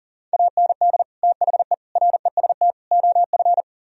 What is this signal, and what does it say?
Morse code, heard as a keyed tone, about 50 wpm, 700 Hz. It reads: AND THE REST OF
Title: Speed-Racing - Sets of 4 Words 40wpm